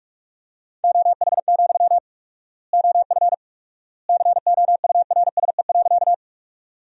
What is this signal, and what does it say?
Morse code, heard as a keyed tone, about 46 wpm, 690 Hz. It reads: OH, OF COURSE.
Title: 4x Word-Spacing - Sentences from Top 500 Words - 45wpm